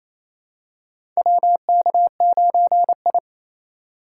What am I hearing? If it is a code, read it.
Morse code, heard as a keyed tone, about 28 wpm, 700 Hz. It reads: WK9I
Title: USA Call Signs - 4 Characters - 2x Character Spacing - 28wpm